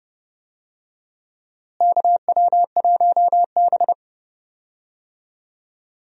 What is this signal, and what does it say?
Morse code, heard as a keyed tone, about 30 wpm, 700 Hz. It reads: KW1B
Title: USA Call Signs - 4 Characters - Repeated 3x - 30wpm